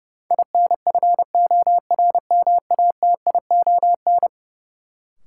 Morse code, heard as a keyed tone, about 30 wpm, 700 Hz. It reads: INFORMATION
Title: Sentences from Top 600 Words - New Word Drill - 30wpm